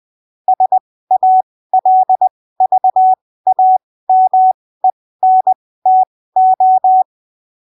Morse code, heard as a keyed tone, about 20 wpm, spaced as for 15 wpm, 750 Hz. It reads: SALVAMENTO
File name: arquivos / texto / texto 03 20ppm